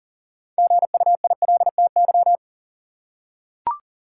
Morse code, heard as a keyed tone, about 40 wpm, 700 Hz. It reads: GUILTY
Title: Six Letter Words - 40wpm